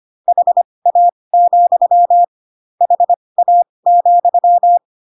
Morse code, heard as a keyed tone, about 25 wpm, spaced as for 20 wpm, 700 Hz. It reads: HA, HA,
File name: Pinocchio / Pinocchio ch1 15 25 ik1hge